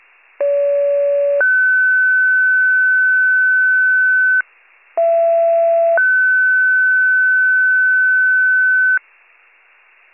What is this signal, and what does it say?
Two-tone page, heard as a one-second tone, then a three-second tone, about 570 then 1500 Hz, about 670 then 1500 Hz. It is heard from 0.4 to 9.0 s.